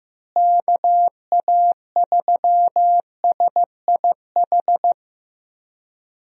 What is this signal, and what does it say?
Morse code, heard as a keyed tone, about 15 wpm, 700 Hz. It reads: KA3SIH
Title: USA Call Signs - 6 Characters - Repeated 3x - 15wpm